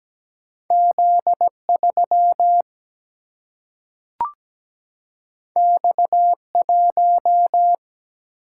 Morse code, heard as a keyed tone, about 17 wpm, 700 Hz. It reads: Z3 X1
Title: Single-Letter Single-Number 17wpm